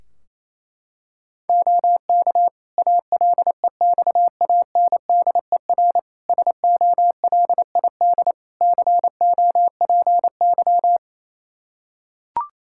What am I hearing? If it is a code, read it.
Morse code, heard as a keyed tone, about 28 wpm, 700 Hz. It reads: OK ALEXANDER SOLID COPY
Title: K1USN SST QSO Practice 28wpm